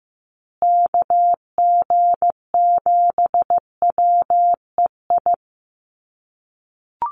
Morse code, heard as a keyed tone, about 15 wpm, 700 Hz. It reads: KG7WEI
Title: USA Call Signs Repeated 2x 15wpm